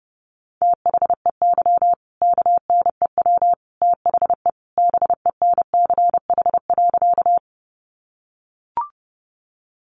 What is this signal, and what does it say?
Morse code, heard as a keyed tone, about 30 wpm, 700 Hz. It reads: THEY KNEW THE BENCH.
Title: Sentences from Top 1400 Words 30wpm